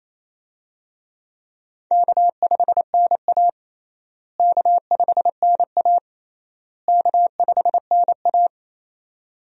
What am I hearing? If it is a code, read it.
Morse code, heard as a keyed tone, about 27 wpm, 700 Hz. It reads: K5NA K5NA K5NA
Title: USA Call Signs - 4 Characters - Repeated 3x - 28wpm